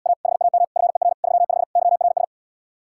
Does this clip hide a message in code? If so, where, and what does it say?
Yes: Morse code from 0.1 to 2.2 s, heard as a keyed tone, about 79 wpm, 700 Hz. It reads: A HAM FER 35 YRS